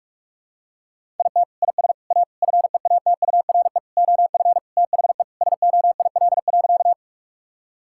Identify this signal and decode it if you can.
Morse code, heard as a keyed tone, about 46 wpm, 700 Hz. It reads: IT IS A FEATURE OF THE SOIL.
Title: Sentences from Top 1000 Words - Review Part 5 - 45wpm